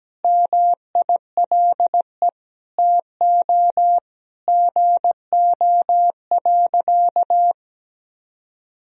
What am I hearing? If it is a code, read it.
Morse code, heard as a keyed tone, about 17 wpm, 700 Hz. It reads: MILE TO GO.
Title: Sentences from Top 1000 Words - Review Part 1 - 17wpm